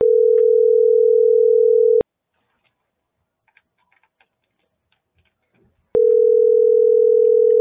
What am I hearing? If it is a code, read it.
Ringback tone (North America), 440 + 480 Hz.